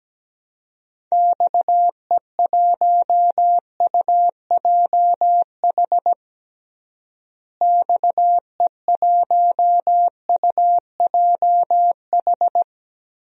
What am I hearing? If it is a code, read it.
Morse code, heard as a keyed tone, about 17 wpm, 700 Hz. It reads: XE1UJH XE1UJH
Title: DXCC-100 Call Signs - 6 Characters - Repeated 3x - 17wpm